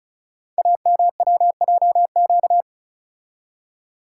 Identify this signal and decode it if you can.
Morse code, heard as a keyed tone, about 36 wpm, 700 Hz. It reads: AMWJQ
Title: Code Groups - 5 Letters - 35wpm